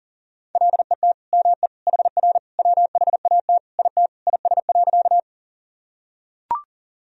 Morse code, heard as a keyed tone, about 40 wpm, 700 Hz. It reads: LET ME HR WHAT IT IS.
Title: Sentences from Top 300 Words and Abbreviations 40wpm